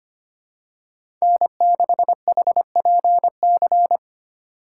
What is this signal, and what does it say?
Morse code, heard as a keyed tone, about 25 wpm, 700 Hz. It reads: N6HPC
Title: USA Call Signs - 5 Characters - 4x Character Spacing - 25wpm